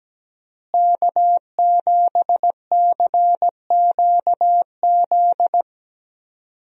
Morse code, heard as a keyed tone, about 17 wpm, 700 Hz. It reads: K7CQZ
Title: USA Call Signs - 5 Characters - Repeated 3x - 17wpm